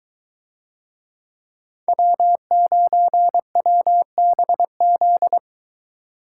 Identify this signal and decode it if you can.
Morse code, heard as a keyed tone, about 22 wpm, 700 Hz. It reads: W9WBZ